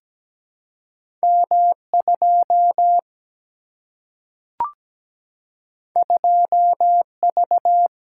Morse code, heard as a keyed tone, about 17 wpm, 700 Hz. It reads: M2 2V
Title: Single-Letter Single-Number 17wpm